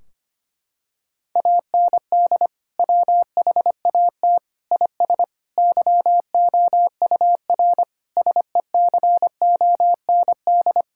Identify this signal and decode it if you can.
Morse code, heard as a keyed tone, about 25 wpm, 700 Hz. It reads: AND WHAT IS YOUR SECOND
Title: Sentences from Top 1600 Words 25wpm